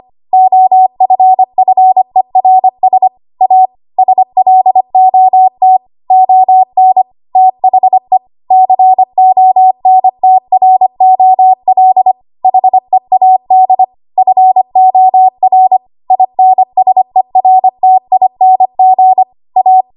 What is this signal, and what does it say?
Morse code, heard as a keyed tone, about 25 wpm, 750 Hz. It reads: OFFERS A SLOT ON THE CONTROL HEAD FOR INSERTING A
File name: Archive / 25 WPM / 25WPM